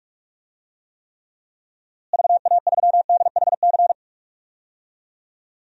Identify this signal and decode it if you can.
Morse code, heard as a keyed tone, about 45 wpm, 700 Hz. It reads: VA3DHC
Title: DXCC-100 Call Signs - 6 Characters - Repeated 2x - 45wpm